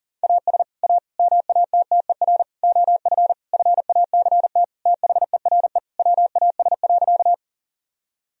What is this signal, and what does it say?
Morse code, heard as a keyed tone, about 40 wpm, 700 Hz. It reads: AS A MATTER OF FACT THERE WAS.
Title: Sentences from Top 1000 Words - Review Part 3 - 40wpm